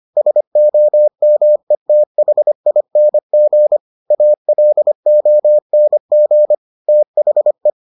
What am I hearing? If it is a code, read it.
Morse code, heard as a keyed tone, about 25 wpm, 600 Hz. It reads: SOMETHING ALONG THE